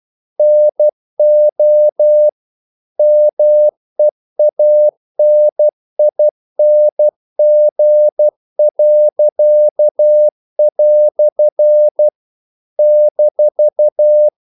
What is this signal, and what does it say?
Morse code, heard as a keyed tone, about 12 wpm, 600 Hz. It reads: NO MEANING." -
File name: CW / Practice / EnglishQuotations0312